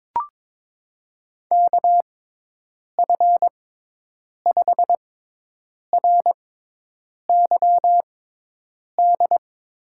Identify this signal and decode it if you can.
Morse code, heard as a keyed tone, about 22 wpm, spaced as for 8 wpm, 700 Hz. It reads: KF5RYD
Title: USA Call Signs - 6 Characters - 6x Character Spacing - 22wpm